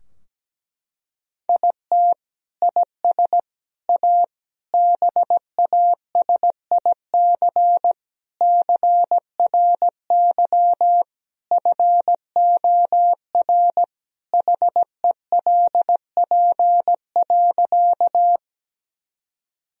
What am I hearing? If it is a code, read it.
Morse code, heard as a keyed tone, about 17 wpm, 700 Hz. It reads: IT IS A BASIC CRY FOR HELP.
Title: Sentences from Top 1000 Words - Review Part 3 - 17wpm